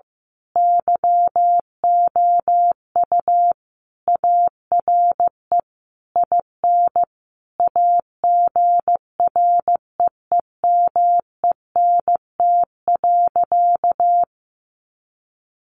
Morse code, heard as a keyed tone, about 14 wpm, 700 Hz. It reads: YOU ARE IN AGREEMENT.
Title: Sentences from Top 1500 Words 15wpm